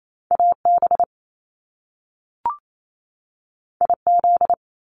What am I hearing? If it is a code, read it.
Morse code, heard as a keyed tone, about 28 wpm, 700 Hz. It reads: AB IZ
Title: Top Two Letter Combinations in Words 28wpm